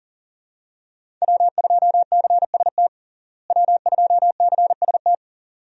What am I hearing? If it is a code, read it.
Morse code, heard as a keyed tone, about 37 wpm, 700 Hz. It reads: W2CST W2CST
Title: USA Call Signs - 5 Characters - Repeated 2x - 40wpm